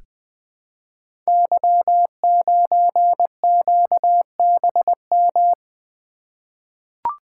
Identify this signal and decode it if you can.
Morse code, heard as a keyed tone, about 20 wpm, 700 Hz. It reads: Y9QBM
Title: Code Groups - 5 Letter:Numbers - 20wpm